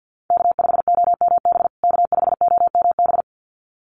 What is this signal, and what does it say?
Morse code, heard as a keyed tone, about 50 wpm, 700 Hz. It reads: K5OMB K5OMB
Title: USA Call Signs Repeated 2x 50wpm